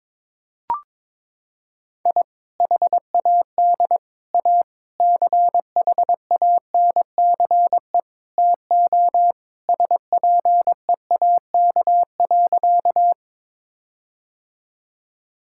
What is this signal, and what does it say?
Morse code, heard as a keyed tone, about 22 wpm, 700 Hz. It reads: I HAD A CHANCE TO SPEAK.
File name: practice / 22wpm / Sentences with 4-Word Phrases - 22wpm